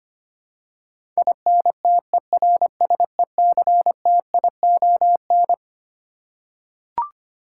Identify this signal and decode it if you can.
Morse code, heard as a keyed tone, about 25 wpm, 700 Hz. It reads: INTERSECTION
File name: Twelve Letter Words - 25wpm